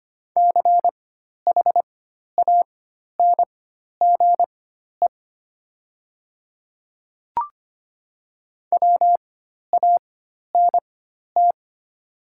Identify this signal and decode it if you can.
Morse code, heard as a keyed tone, about 25 wpm, spaced as for 12 wpm, 700 Hz. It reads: CHANGE WANT
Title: 4x Character Spacing - Top 200 Words - 25wpm